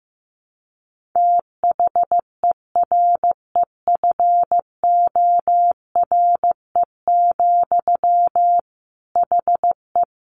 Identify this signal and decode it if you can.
Morse code, heard as a keyed tone, about 15 wpm, 700 Hz. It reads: THEREFORE, HE
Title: Sentences from Top 800 Words 15wpm